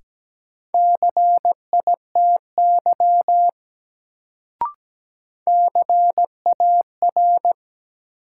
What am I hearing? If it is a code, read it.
Morse code, heard as a keyed tone, about 17 wpm, 700 Hz. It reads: CITY CAR
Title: Sentences from Top 300 Words - New Word Drill - 17wpm